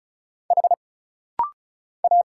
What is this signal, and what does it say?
Morse code, heard as a keyed tone, about 33 wpm, 700 Hz. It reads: H A
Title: Single Letter-Number Alt - Rapid-Fire 35wpm